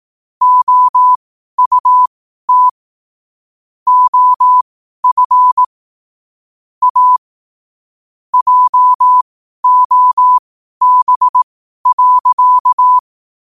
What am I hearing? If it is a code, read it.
Morse code, heard as a keyed tone, about 18 wpm, spaced as for 12 wpm, 1000 Hz. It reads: OUT OF A JOB.
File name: qotdfiles / 12wpm